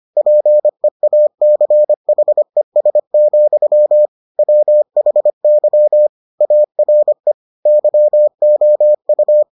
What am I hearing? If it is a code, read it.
Morse code, heard as a keyed tone, about 25 wpm, 600 Hz. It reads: PEACHES, WHY ARE YOU